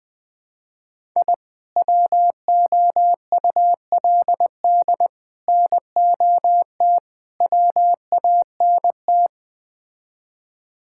Morse code, heard as a keyed tone, about 21 wpm, 700 Hz. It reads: I WOULD NOT WANT
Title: Build A Phrase - 5-Word Phrases - 20wpm